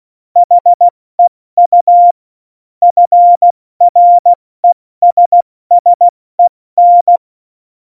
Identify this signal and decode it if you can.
Morse code, heard as a keyed tone, about 16 wpm, 700 Hz. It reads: HEU FRESSEN
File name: Morsekurs / Witze / Witz 26. Dezember 2017, 70 F80